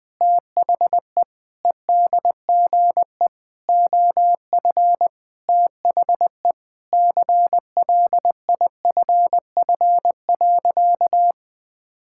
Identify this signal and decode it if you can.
Morse code, heard as a keyed tone, about 20 wpm, 700 Hz. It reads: THE EDGE OF THE CLIFF.
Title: Sentences with 5-Word Phrases - Spoken Beginning - 20wpm